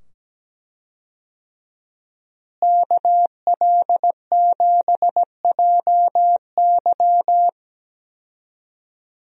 Morse code, heard as a keyed tone, about 17 wpm, 700 Hz. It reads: KL7JY
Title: DXCC-100 Call Signs - 5 Characters - Repeated 2x - 17wpm